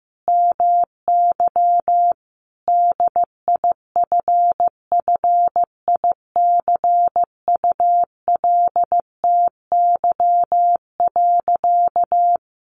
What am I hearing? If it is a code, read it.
Morse code, heard as a keyed tone, about 15 wpm, 700 Hz. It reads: MY DIFFICULTY.
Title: Sentences from Top 1500 Words 15wpm